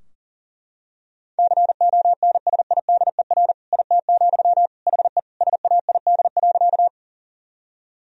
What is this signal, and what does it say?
Morse code, heard as a keyed tone, about 37 wpm, 700 Hz. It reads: CONSIDER IT, HE SAID.
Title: Sentences from Top 1000 Words - Review Part 4 - 40wpm